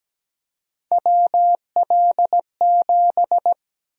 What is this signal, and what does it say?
Morse code, heard as a keyed tone, about 17 wpm, 700 Hz. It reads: WL7
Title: USA Call Sign Prefixes - Repeated 2x - 17wpm